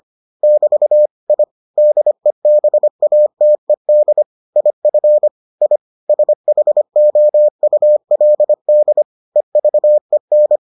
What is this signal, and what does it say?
Morse code, heard as a keyed tone, about 25 wpm, 600 Hz. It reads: <BT> I DEBATED IF I SHOULD EVEN